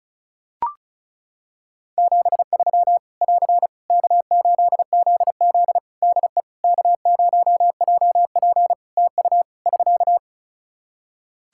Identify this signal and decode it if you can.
Morse code, heard as a keyed tone, about 36 wpm, 700 Hz. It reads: 73 <AR> K8ZZ DE K0JP TU <SK>
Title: Practice QSO - Exchange 4 - Ending - 35wpm